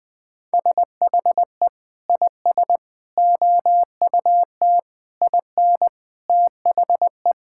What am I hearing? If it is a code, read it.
Morse code, heard as a keyed tone, about 20 wpm, 700 Hz. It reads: SHE IS OUT IN THE
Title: Sentences with 3-Word Phrases - Spoken Beginning - 20wpm